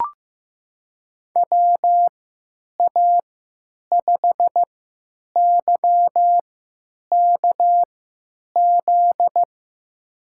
Morse code, heard as a keyed tone, about 15 wpm, spaced as for 9 wpm, 700 Hz. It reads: WA5YKZ
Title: USA Call Signs - 3x Character Spacing - 15wpm